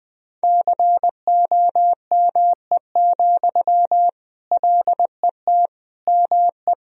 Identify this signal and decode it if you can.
Morse code, heard as a keyed tone, about 20 wpm, 700 Hz. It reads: COME, LET ME
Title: Sentences from Top 500 Words and Abbreviations 20wpm